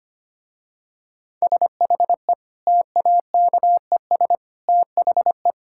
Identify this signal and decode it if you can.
Morse code, heard as a keyed tone, about 25 wpm, 700 Hz. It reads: SHE TAKES THE
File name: practice / 25wpm / Difficult Sentences with Hints - 25wpm